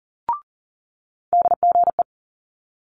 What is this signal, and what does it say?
Morse code, heard as a keyed tone, about 41 wpm, 700 Hz. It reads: DGE
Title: Multi-letter Phonemes Quick Fire 40wpm